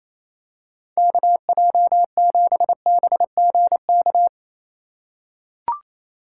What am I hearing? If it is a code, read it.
Morse code, heard as a keyed tone, about 28 wpm, 700 Hz. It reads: KJ7BGK
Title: USA Call Signs - 6 Characters - 8x Character Spacing - 28wpm